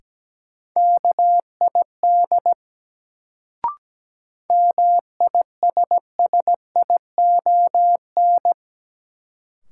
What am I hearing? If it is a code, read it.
Morse code, heard as a keyed tone, about 17 wpm, 700 Hz. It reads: KID MISSION
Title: Sentences from Top 1600 Words - New Word Drill Alt - 17wpm